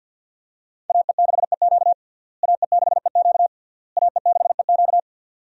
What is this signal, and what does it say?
Morse code, heard as a keyed tone, about 52 wpm, 700 Hz. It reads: AE6EQ AE6EQ AE6EQ
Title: USA Call Signs - 5 Characters - Repeated 3x - 50wpm